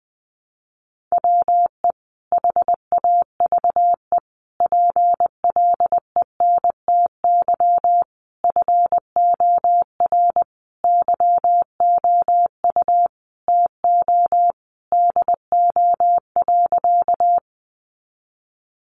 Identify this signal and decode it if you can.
Morse code, heard as a keyed tone, about 19 wpm, 700 Hz. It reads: WE HAVE PLENTY FOR YOU TO DO.
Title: Sentences from Top 1300 Words 20wpm